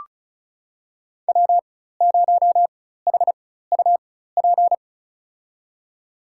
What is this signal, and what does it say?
Morse code, heard as a keyed tone, about 36 wpm, spaced as for 16 wpm, 700 Hz. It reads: W0HUP
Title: USA Call Signs - 4x Character Spacing - 35wpm